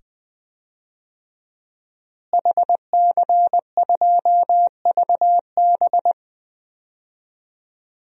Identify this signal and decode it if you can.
Morse code, heard as a keyed tone, about 21 wpm, 700 Hz. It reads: HC2VB
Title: DXCC-100 Call Signs - 5 Characters - Repeated 2x - 20wpm